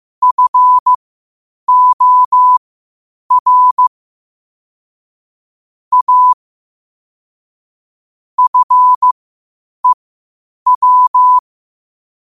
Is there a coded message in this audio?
Yes: Morse code from 0.2 to 11.4 s, heard as a keyed tone, about 15 wpm, spaced as for 8 wpm, 1000 Hz. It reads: FOR A FEW